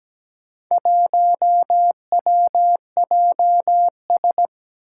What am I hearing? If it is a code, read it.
Morse code, heard as a keyed tone, about 17 wpm, 700 Hz. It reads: 1WJS
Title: Four Challenging Letter-Number Combinations 17wpm